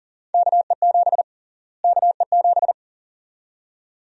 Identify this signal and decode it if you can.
Morse code, heard as a keyed tone, about 40 wpm, 700 Hz. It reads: KE7 KE7
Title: USA Call Sign Prefixes - Repeated 2x - 40wpm